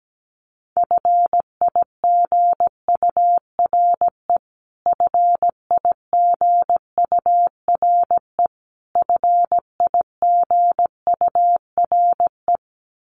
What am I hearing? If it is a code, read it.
Morse code, heard as a keyed tone, about 17 wpm, 700 Hz. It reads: FIGURE FIGURE FIGURE
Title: Top 500 Words Repeated 17wpm